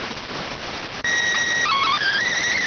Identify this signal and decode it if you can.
SSTV (slow-scan television), PD 290.